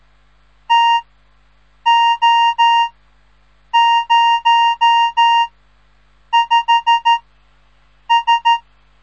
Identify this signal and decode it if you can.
Morse code, heard as a keyed tone, about 13 wpm, spaced as for 7 wpm, 930 Hz. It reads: TO05S
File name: radiomorningstar / morse / les 3